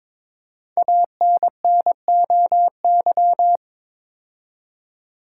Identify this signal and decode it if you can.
Morse code, heard as a keyed tone, about 22 wpm, 700 Hz. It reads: ANNOY
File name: 5 Letter Words - First 3 Letters Given - 22wpm